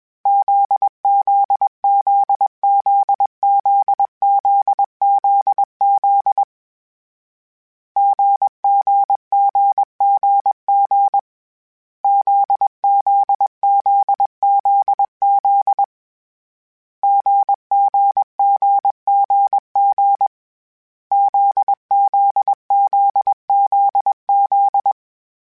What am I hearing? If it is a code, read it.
Morse code, heard as a keyed tone, about 21 wpm, 800 Hz. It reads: ZZZZZZZZ GGGGG ZZZZZ GGGGG ZZZZZ
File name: Difficult Combos / Difficult Combos-mp3s / GZ GZ7 @20